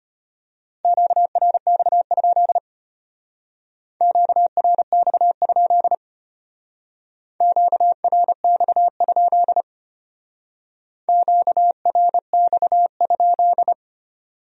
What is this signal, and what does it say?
Morse code, heard as a keyed tone, about 32 wpm, 700 Hz. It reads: QRX? QRX? QRX? QRX?